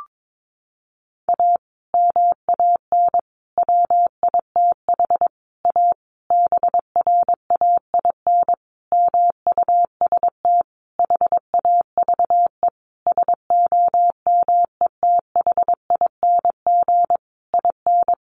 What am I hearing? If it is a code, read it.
Morse code, heard as a keyed tone, about 22 wpm, 700 Hz. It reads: A MAN WITH A BRAIN MUST HAVE SOMETHING IN
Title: Sentences from Top 1100 Words 22wpm